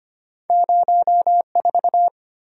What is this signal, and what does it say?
Morse code, heard as a keyed tone, about 25 wpm, 700 Hz. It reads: 04
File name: Two Letter-Number Combinations - 25wpm